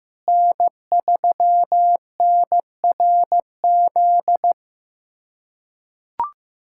Morse code, heard as a keyed tone, about 16 wpm, 700 Hz. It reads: N3NRZ
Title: USA Callsign DE Callsign - 5 Characters - 15wpm